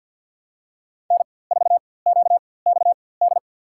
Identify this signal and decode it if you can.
Morse code, heard as a keyed tone, about 53 wpm, spaced as for 24 wpm, 700 Hz. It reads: N4QXD
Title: USA Call Signs - 5 Characters - 4x Character Spacing - 50wpm